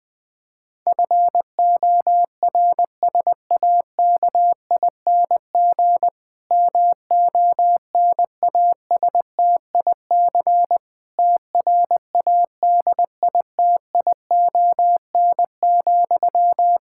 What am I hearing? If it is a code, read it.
Morse code, heard as a keyed tone, about 20 wpm, 700 Hz. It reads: FORSAKING MONASTIC TRADITION,
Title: Pangram Challenge v2 - 20wpm